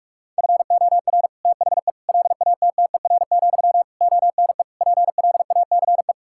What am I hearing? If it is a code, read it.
Morse code, heard as a keyed tone, about 46 wpm, 700 Hz. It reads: FOR THE LATTER, ONE PLACE
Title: Sentences from Top 1600 Words 45wpm